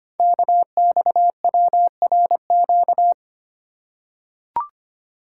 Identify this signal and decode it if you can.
Morse code, heard as a keyed tone, about 25 wpm, 700 Hz. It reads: KXWRQ
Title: Code Groups - 5 Letters - 25wpm